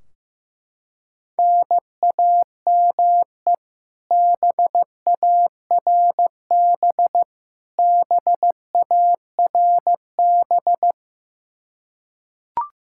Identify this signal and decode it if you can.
Morse code, heard as a keyed tone, about 15 wpm, 700 Hz. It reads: NAME BARB BARB
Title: Practice QSO - Exchange 1 - Beginning - 15wpm